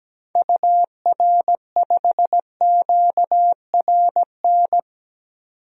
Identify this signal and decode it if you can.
Morse code, heard as a keyed tone, about 17 wpm, 700 Hz. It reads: UR5QRN
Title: DXCC-100 Call Signs - 6 Characters - 17wpm